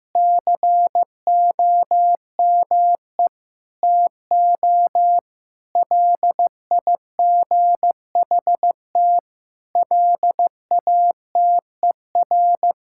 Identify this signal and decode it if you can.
Morse code, heard as a keyed tone, about 15 wpm, 700 Hz. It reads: COME TO LIGHT LATER
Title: Sentences from Top 1000 Words - Review Part 6 - 15wpm